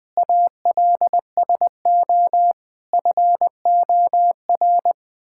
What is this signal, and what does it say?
Morse code, heard as a keyed tone, about 20 wpm, 700 Hz. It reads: ALSO FOR
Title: Sentences from Top 1000 Words - Review Part 6 - 20wpm